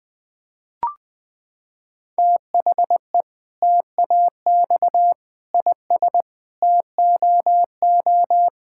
Morse code, heard as a keyed tone, about 20 wpm, 700 Hz. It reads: THE TAX IS TOO
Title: Sentences from Top 1000 Words - Review Part 5 - 20wpm